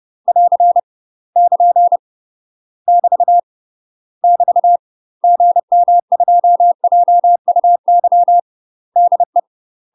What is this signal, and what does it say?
Morse code, heard as a keyed tone, about 30 wpm, 700 Hz. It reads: <AR> <KN> <BT> <BT> GM2JUY DE
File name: UK MASTER QSOs TRAINING/30WPM